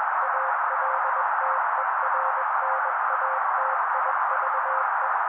Morse code, heard as a keyed tone, about 21 wpm, 530 Hz. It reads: ALTERNATIVE
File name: daily / 20wpm noise